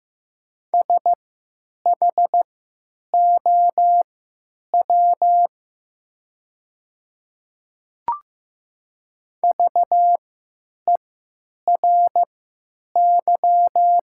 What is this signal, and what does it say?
Morse code, heard as a keyed tone, about 15 wpm, spaced as for 9 wpm, 700 Hz. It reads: SHOW VERY
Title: 3x Character Spacing - Top 200 Words - 15wpm